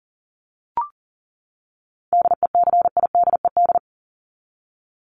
Morse code, heard as a keyed tone, about 40 wpm, 700 Hz. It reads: DECIDED
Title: ED Suffix - Top 100 Words with Base Words 40wpm